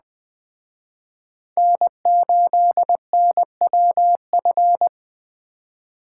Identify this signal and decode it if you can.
Morse code, heard as a keyed tone, about 20 wpm, 700 Hz. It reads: N8NWF
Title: Speed Racing - USA 1x3 Call Signs 17wpm